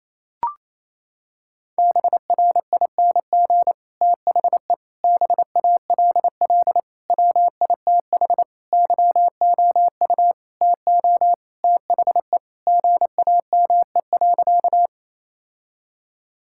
Morse code, heard as a keyed tone, about 28 wpm, 700 Hz. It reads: BRING THE BALL WITH YOU TO THE GAME.
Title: Sentences from Top 1000 Words - Review Part 6 - 28wpm